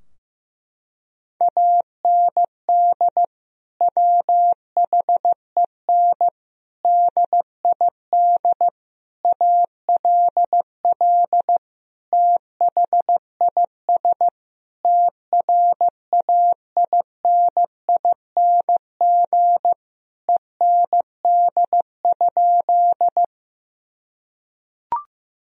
Morse code, heard as a keyed tone, about 16 wpm, 700 Hz. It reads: AND WHEN DID ALL THIS TRAINING END?
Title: Sentences from Top 1000 Words - Review Part 4 - 15wpm